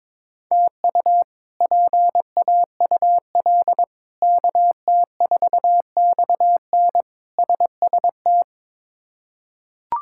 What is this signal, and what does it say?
Morse code, heard as a keyed tone, about 22 wpm, 700 Hz. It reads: TU PAUL KT4XN SST
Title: K1USN SST QSO Practice - 22wpm